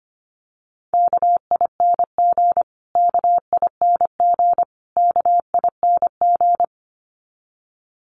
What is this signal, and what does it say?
Morse code, heard as a keyed tone, about 25 wpm, 700 Hz. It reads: KING KING KING
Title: Top 500 Words Repeated 25wpm